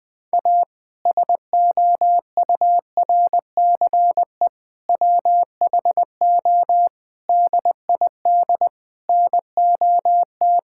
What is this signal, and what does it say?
Morse code, heard as a keyed tone, about 20 wpm, 700 Hz. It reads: A SOURCE WHO DID NOT
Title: Sentences with 8-Word Phrases - 20wpm